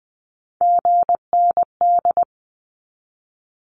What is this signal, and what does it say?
Morse code, heard as a keyed tone, about 20 wpm, 700 Hz. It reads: GND
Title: Abbreviations Repeated 3x - 20wpm